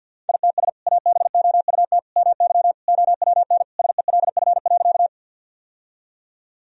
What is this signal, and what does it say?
Morse code, heard as a keyed tone, about 53 wpm, 700 Hz. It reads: ITS ABOUT MY OWN SELF.
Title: Sentences from Top 1000 Words - Review Part 1 - 50wpm